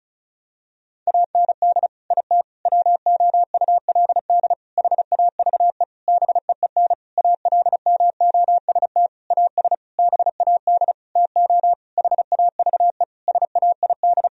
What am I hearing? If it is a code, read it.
Morse code, heard as a keyed tone, about 36 wpm, 700 Hz. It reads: AND IT WOULD HAVE BEEN ALMOST AS BAD TO HAVE SAID
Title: Sentences from Top 1000 Words - Review Part 4 - 35wpm